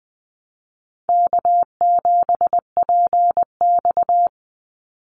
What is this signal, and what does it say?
Morse code, heard as a keyed tone, about 20 wpm, 700 Hz. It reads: K7PX
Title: NA SOTA Activator Callsigns 20wpm